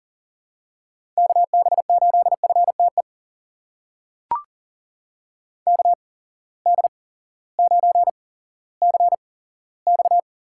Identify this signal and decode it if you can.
Morse code, heard as a keyed tone, about 40 wpm, spaced as for 18 wpm, 700 Hz. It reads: KB8FTE KD9CX
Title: USA Call Signs - 6 Characters - 8x Character Spacing - 40wpm